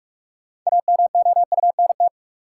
Morse code, heard as a keyed tone, about 42 wpm, 700 Hz. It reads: AMOUNT
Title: Sentences from Top 700 Words - New Word Drill - 45wpm